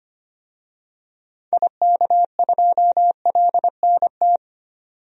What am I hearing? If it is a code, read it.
Morse code, heard as a keyed tone, about 25 wpm, 700 Hz. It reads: IK2LNT
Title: DXCC-100 Call Signs - 6 Characters - 25wpm